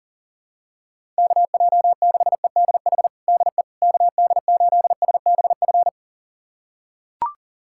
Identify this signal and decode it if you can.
Morse code, heard as a keyed tone, about 40 wpm, 700 Hz. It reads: KJ6EDH DE KD8SBF
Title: USA Callsign DE Callsign - 6 Characters - 40wpm